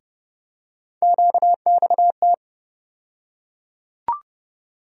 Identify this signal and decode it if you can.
Morse code, heard as a keyed tone, about 30 wpm, 700 Hz. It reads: QXT
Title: USA Call Sign Postfixes - 3 Letters 30wpm